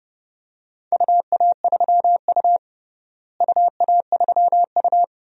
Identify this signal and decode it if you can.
Morse code, heard as a keyed tone, about 30 wpm, 700 Hz. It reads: UA3U UA3U
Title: DXCC-100 Call Signs - 4 Characters - Repeated 2x - 30wpm